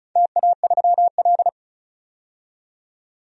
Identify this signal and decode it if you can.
Morse code, heard as a keyed tone, about 35 wpm, 700 Hz. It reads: TA3L
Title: DXCC-100 Call Signs - 4 Characters - 35wpm